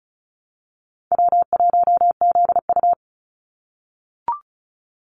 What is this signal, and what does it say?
Morse code, heard as a keyed tone, about 36 wpm, 700 Hz. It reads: W1ZU
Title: NA SOTA Activator Callsigns 35wpm